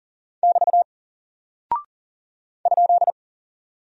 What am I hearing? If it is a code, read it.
Morse code, heard as a keyed tone, about 40 wpm, 700 Hz. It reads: <BT> ?
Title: Single Character (Letter:Number:Punctuation:Prosign) Alt 40wpm